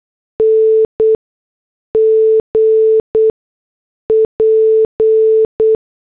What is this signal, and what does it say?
Morse code, heard as a keyed tone, about 8 wpm, 440 Hz. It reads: N G P
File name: morse / headlines